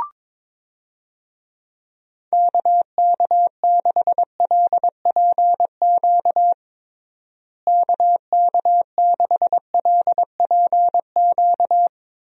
Morse code, heard as a keyed tone, about 22 wpm, 700 Hz. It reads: KK6LPQ KK6LPQ
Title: USA Call Signs - 6 Characters - Repeated 2x - 22wpm